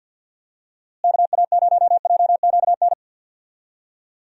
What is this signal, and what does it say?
Morse code, heard as a keyed tone, about 51 wpm, 700 Hz. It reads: KA0JQN
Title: USA Call Signs - 6 Characters - Repeated 3x - 50wpm